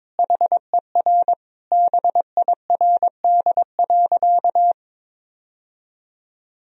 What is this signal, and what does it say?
Morse code, heard as a keyed tone, about 22 wpm, 700 Hz. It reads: HER BIRD.
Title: Sentences from Top 1000 Words - Review Part 2 - 22wpm